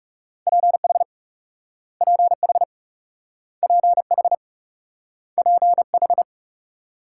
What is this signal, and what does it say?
Morse code, heard as a keyed tone, about 38 wpm, 700 Hz. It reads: PH PH PH PH